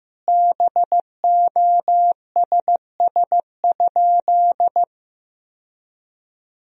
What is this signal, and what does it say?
Morse code, heard as a keyed tone, about 15 wpm, 700 Hz. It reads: BOSS?
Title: Sentences from Top 1000 Words - Review Part 4 - 15wpm